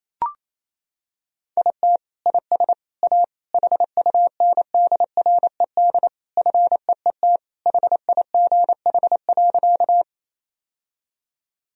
Morse code, heard as a keyed tone, about 28 wpm, 700 Hz. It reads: IT IS A HUNDRED FEET HIGH.
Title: Sentences from Top 1000 Words - Review Part 2 - 28wpm